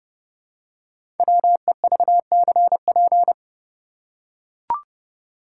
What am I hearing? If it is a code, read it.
Morse code, heard as a keyed tone, about 30 wpm, 700 Hz. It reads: WEVCP
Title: Code Groups - 5 Letter:Numbers - 30wpm